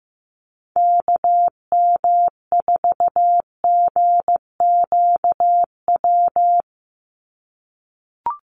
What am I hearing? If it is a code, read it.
Morse code, heard as a keyed tone, about 15 wpm, 700 Hz. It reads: KM4GQW
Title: USA Call Signs 15wpm